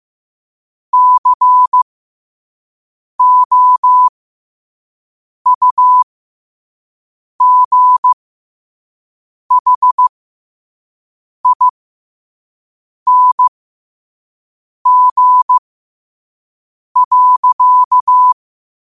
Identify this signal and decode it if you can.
Morse code, heard as a keyed tone, about 15 wpm, spaced as for 5 wpm, 1000 Hz. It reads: COUGHING.
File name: qotdfiles / 5wpm